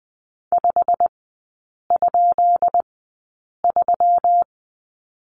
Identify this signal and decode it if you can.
Morse code, heard as a keyed tone, about 20 wpm, spaced as for 8 wpm, 700 Hz. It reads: 5?3
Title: 172 Lesson 28e - Number 9 - Three Characters 20wpm